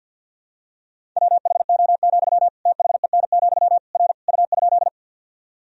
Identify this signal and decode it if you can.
Morse code, heard as a keyed tone, about 48 wpm, 700 Hz. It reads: WHO, THEN, R U?
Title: Sentences from Top 100 Words and Abbreviations 50wpm